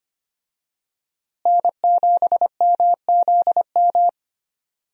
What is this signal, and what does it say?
Morse code, heard as a keyed tone, about 25 wpm, 700 Hz. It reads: N7MZM
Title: USA Call Signs - 5 Characters - Repeated 3x - 25wpm